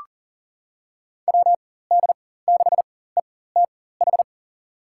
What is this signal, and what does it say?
Morse code, heard as a keyed tone, about 40 wpm, spaced as for 19 wpm, 700 Hz. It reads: WD6ETH
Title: USA Call Signs - 6 Characters - 4x Character Spacing - 40wpm